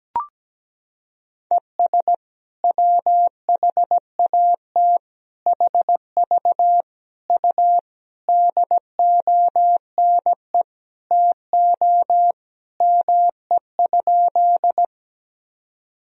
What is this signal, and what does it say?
Morse code, heard as a keyed tone, about 17 wpm, 700 Hz. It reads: ES WHAT HV U DONE TO ME?
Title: Sentences from Top 500 Words and Abbreviations 17wpm